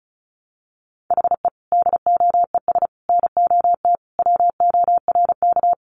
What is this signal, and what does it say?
Morse code, heard as a keyed tone, about 34 wpm, 700 Hz. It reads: HE DOES NOT WORK
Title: Additional Easy Phrases 35wpm